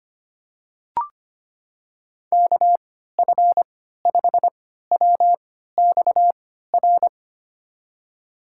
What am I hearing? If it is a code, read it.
Morse code, heard as a keyed tone, about 25 wpm, spaced as for 14 wpm, 700 Hz. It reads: KF5WXR
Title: USA Call Signs - 6 Characters - 3x Character Spacing - 25wpm